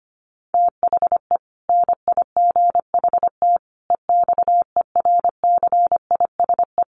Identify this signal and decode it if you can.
Morse code, heard as a keyed tone, about 25 wpm, 700 Hz. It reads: THE NIGHT EXERCISE
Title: Sentences from Top 1400 Words 25wpm